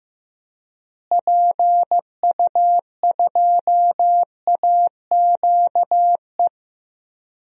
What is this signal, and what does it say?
Morse code, heard as a keyed tone, about 15 wpm, 700 Hz. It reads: PU2AQE
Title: DXCC-100 Call Signs - 6 Characters - Repeated 2x - 15wpm